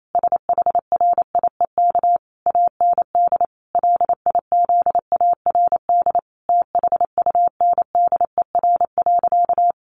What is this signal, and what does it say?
Morse code, heard as a keyed tone, about 28 wpm, 700 Hz. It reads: SHRIEK AND LIZARD THUNDER.
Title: A Sound Of Thunder 28wpm